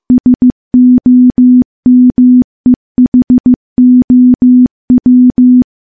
Morse code, heard as a keyed tone, about 15 wpm, 250 Hz. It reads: SOMEHOW
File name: morse / morse-training-15WPM